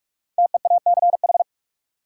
Morse code, heard as a keyed tone, about 43 wpm, 700 Hz. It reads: TEACH
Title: Sentences from Top 500 Words - New Word Drill - 45wpm